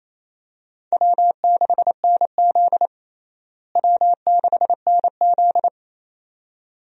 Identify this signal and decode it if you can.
Morse code, heard as a keyed tone, about 28 wpm, 700 Hz. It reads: W6NZ W6NZ
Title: USA Call Signs - 4 Characters - Repeated 2x - 28wpm